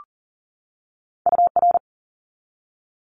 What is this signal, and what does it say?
Morse code, heard as a keyed tone, about 37 wpm, 700 Hz. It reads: UR
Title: Abbreviations 40wpm